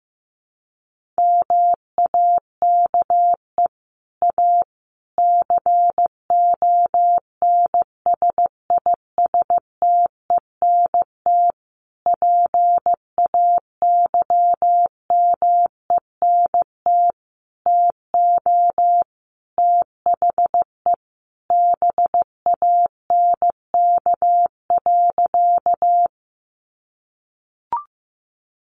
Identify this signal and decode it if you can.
Morse code, heard as a keyed tone, about 15 wpm, 700 Hz. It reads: MAKE A CONSISTENT PAYMENT TO THE BANK.
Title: Sentences from Top 1500 Words 15wpm